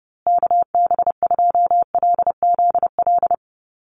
Morse code, heard as a keyed tone, about 30 wpm, 700 Hz. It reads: KB2LZL
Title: USA Call Signs Repeated 3x 30wpm